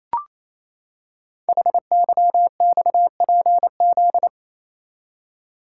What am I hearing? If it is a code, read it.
Morse code, heard as a keyed tone, about 27 wpm, 700 Hz. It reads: HYXPZ
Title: Code Groups - 5 Letters - 28wpm